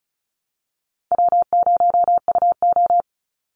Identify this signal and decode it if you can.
Morse code, heard as a keyed tone, about 34 wpm, 700 Hz. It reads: W0UO
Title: Call Signs Repeated 2x - 35wpm